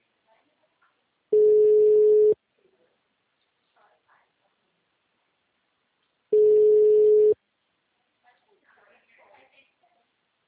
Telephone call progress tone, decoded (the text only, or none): ringback tone